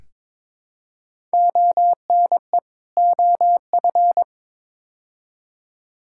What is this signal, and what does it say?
Morse code, heard as a keyed tone, about 22 wpm, 700 Hz. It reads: ONE OF
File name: practice / 22wpm / Build A Phrase - 5-Word Phrases - 22wpm